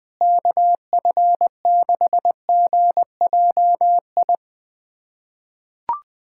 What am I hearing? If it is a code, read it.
Morse code, heard as a keyed tone, about 20 wpm, 700 Hz. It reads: KF6GJI
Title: USA Callsign DE Callsign 20wpm